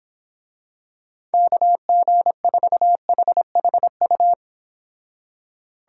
Morse code, heard as a keyed tone, about 26 wpm, 700 Hz. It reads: KG4HHU
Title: Speed Racing - USA 2x3 Call Signs 22wpm